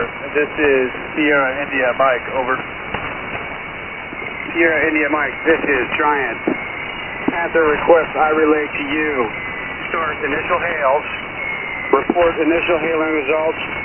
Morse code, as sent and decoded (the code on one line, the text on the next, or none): none